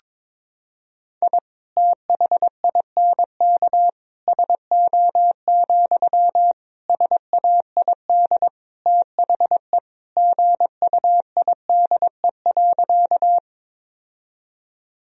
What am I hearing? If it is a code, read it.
Morse code, heard as a keyed tone, about 22 wpm, 690 Hz. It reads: I THINK SO, SAID THE GUIDE.
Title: Sentences from Top 1000 Words - Review Part 3 - 22wpm